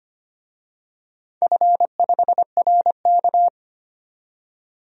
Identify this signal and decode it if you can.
Morse code, heard as a keyed tone, about 25 wpm, 700 Hz. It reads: F5RK
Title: DXCC-100 Call Signs - 4 Characters - 25wpm